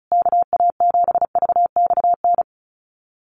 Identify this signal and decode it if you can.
Morse code, heard as a keyed tone, about 34 wpm, 700 Hz. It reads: KA7VXN
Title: USA Call Signs Repeated 3x 35wpm